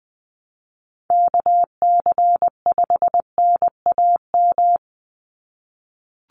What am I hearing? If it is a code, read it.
Morse code, heard as a keyed tone, about 20 wpm, 700 Hz. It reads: KC5NAM
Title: USA Call Signs 20wpm